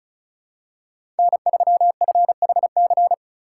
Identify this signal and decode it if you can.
Morse code, heard as a keyed tone, about 34 wpm, 700 Hz. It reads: N3FHC
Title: USA Call Signs - 5 Characters - 4x Character Spacing - 35wpm